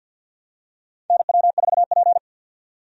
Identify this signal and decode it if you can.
Morse code, heard as a keyed tone, about 47 wpm, 700 Hz. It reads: NW4P
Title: USA Call Signs - 4 Characters - 50wpm